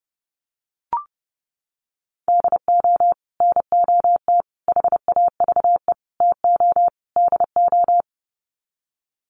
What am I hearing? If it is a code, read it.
Morse code, heard as a keyed tone, about 30 wpm, 700 Hz. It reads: DO NOT HAVE TO DO
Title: Sets of 5 Words 30wpm